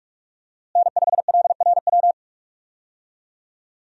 Morse code, heard as a keyed tone, about 41 wpm, 700 Hz. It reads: N5LRW
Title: USA Call Signs - 5 Characters - Repeated 2x - 45wpm